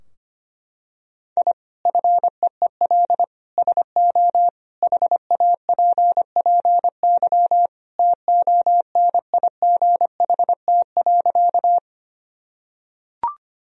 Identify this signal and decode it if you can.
Morse code, heard as a keyed tone, about 25 wpm, 700 Hz. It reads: I FEEL SO HAPPY TONIGHT.
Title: Sentences from Top 1000 Words - Review Part 3 - 25wpm